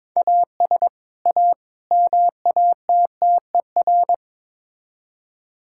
Morse code, heard as a keyed tone, about 22 wpm, 700 Hz. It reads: AS A MATTER
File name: practice / 22wpm / Build A Phrase - 5-Word Phrases - 22wpm